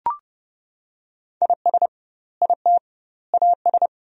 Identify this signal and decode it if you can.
Morse code, heard as a keyed tone, about 30 wpm, 700 Hz. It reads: IS IT AS
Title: 2x Word-Spacing - Sentences from Top 200 Words - 30wpm